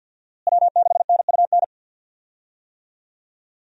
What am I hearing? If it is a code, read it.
Morse code, heard as a keyed tone, about 47 wpm, 700 Hz. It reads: W6NUN
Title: USA Call Signs - 5 Characters - Repeated 2x - 50wpm